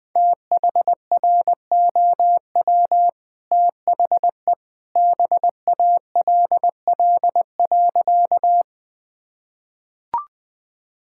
Morse code, heard as a keyed tone, about 21 wpm, 700 Hz. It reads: THROW THE BALL.
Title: Easy Sentences with Hints - 20wpm